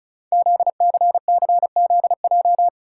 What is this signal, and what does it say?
Morse code, heard as a keyed tone, about 36 wpm, 700 Hz. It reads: ZCCZJ
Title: Code Groups - 5 Letters - 35wpm